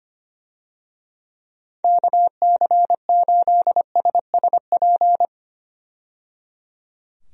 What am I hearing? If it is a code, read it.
Morse code, heard as a keyed tone, about 25 wpm, 700 Hz. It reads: KC8SSP
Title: USA Call Signs - 6 Characters - 25wpm